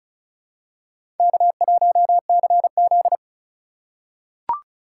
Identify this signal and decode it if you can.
Morse code, heard as a keyed tone, about 34 wpm, 690 Hz. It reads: K1CZ
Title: USA Call Signs - 4x Character Spacing - 35wpm